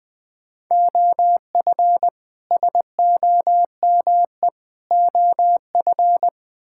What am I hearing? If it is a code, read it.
Morse code, heard as a keyed tone, about 20 wpm, 700 Hz. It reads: OF SOME OF
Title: Build A Phrase - 4-Word Phrases - 20wpm